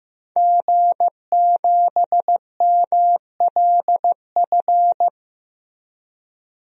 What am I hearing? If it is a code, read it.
Morse code, heard as a keyed tone, about 16 wpm, 700 Hz. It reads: G7MLF
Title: DXCC-100 Call Signs - 5 Characters - Repeated 3x - 15wpm